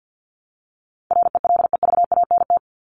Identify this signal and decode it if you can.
Morse code, heard as a keyed tone, about 50 wpm, 700 Hz. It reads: RELEVANT
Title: Top 1001-1250 Words 50wpm